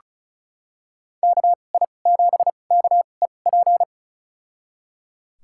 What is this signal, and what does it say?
Morse code, heard as a keyed tone, about 36 wpm, spaced as for 25 wpm, 700 Hz. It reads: KI7KEP
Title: USA Call Signs - 6 Characters - 2x Character Spacing - 35wpm